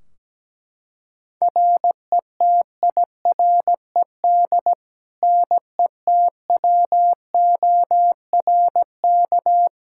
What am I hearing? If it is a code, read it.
Morse code, heard as a keyed tone, about 17 wpm, 700 Hz. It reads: RETIRED NETWORK
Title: Practice QSO - Exchange 3 - More Info - 17wpm